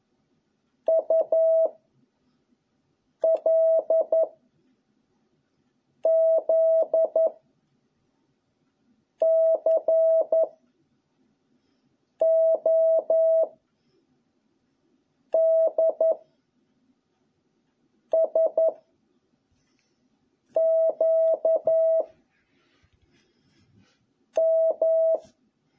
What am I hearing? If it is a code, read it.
Morse code, heard as a keyed tone, about 11 wpm, 650 Hz. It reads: U L Z C O D S Q M